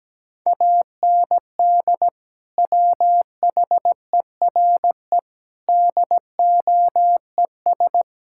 Morse code, heard as a keyed tone, about 17 wpm, 700 Hz. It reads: AND WHERE DOES
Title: Sentences from Top 1000 Words - Review Part 5 - 17wpm